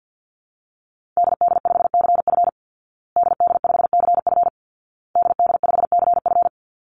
Morse code, heard as a keyed tone, about 48 wpm, 700 Hz. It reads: DD5CF DD5CF DD5CF
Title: Call Signs Repeated 2x - 50wpm